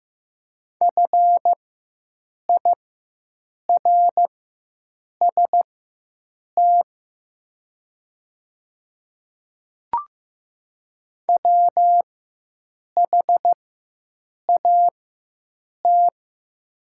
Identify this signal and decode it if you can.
Morse code, heard as a keyed tone, about 15 wpm, spaced as for 7 wpm, 700 Hz. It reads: FIRST WHAT
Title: 4x Character Spacing - Top 100 Words - 15wpm